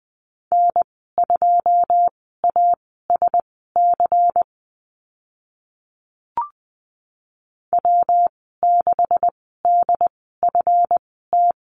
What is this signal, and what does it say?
Morse code, heard as a keyed tone, about 20 wpm, spaced as for 14 wpm, 700 Hz. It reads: N2ASC W6DFT
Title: 140 Lesson 23g - Letter f - Call Signs 20wpm